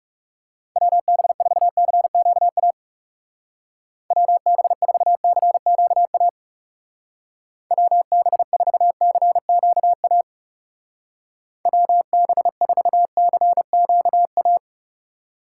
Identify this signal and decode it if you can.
Morse code, heard as a keyed tone, about 36 wpm, 700 Hz. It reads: WB4CQA WB4CQA WB4CQA WB4CQA
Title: Speed Racing - USA 2x3 Call Signs 30wpm